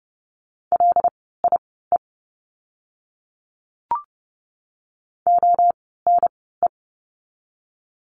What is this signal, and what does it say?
Morse code, heard as a keyed tone, about 30 wpm, spaced as for 17 wpm, 700 Hz. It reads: LIE ONE
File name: Lesson 16f - Letter c - Words 30wpm